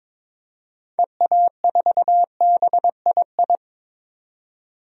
Morse code, heard as a keyed tone, about 22 wpm, 700 Hz. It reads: EA4BII
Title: DXCC-100 Call Signs - 6 Characters - 22wpm